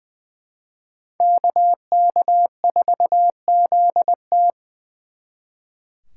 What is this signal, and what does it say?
Morse code, heard as a keyed tone, about 24 wpm, 700 Hz. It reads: KK4ZT
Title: Speed Racing - USA Call Signs 20wpm